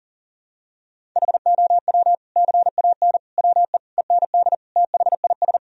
Morse code, heard as a keyed tone, about 40 wpm, 700 Hz. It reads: HOW CAN WE END THIS
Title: Sentences from Top 1000 Words - Review Part 2 - 40wpm